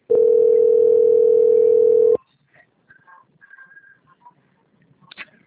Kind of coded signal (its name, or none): telephone call progress tone